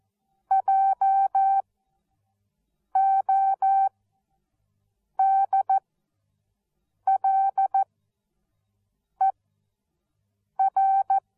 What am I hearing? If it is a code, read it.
Morse code, heard as a keyed tone, about 14 wpm, spaced as for 5 wpm, 780 Hz. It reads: JODLER